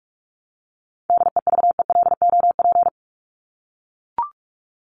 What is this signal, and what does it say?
Morse code, heard as a keyed tone, about 44 wpm, 700 Hz. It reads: DEVELOP
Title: ED Suffix - Top 100 Words with Base Words 45wpm